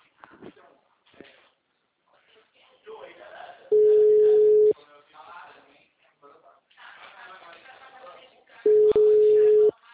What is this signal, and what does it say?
Ringback tone (Europe, 425 Hz), with speech.
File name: out- -2015- - - .772